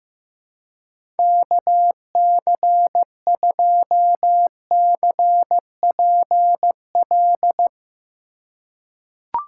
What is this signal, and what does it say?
Morse code, heard as a keyed tone, about 15 wpm, 700 Hz. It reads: KC2CPL
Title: USA Call Signs - 6 Characters - 3x Character Spacing - 15wpm